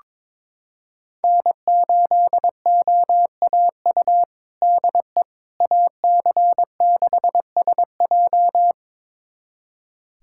Morse code, heard as a keyed tone, about 22 wpm, 700 Hz. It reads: N8OAU DE AC6SJ
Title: USA Callsign DE Callsign - 5 Characters - 22wpm